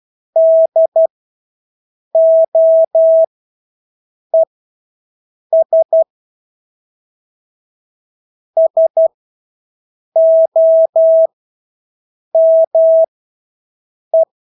Morse code, heard as a keyed tone, about 12 wpm, spaced as for 6 wpm, 650 Hz. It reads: DOES SOME
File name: Engelske ord 12-6 wpm